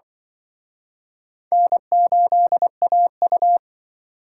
Morse code, heard as a keyed tone, about 25 wpm, 700 Hz. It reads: N8AU
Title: Speed Racing - USA 1x2 Call Signs 20wpm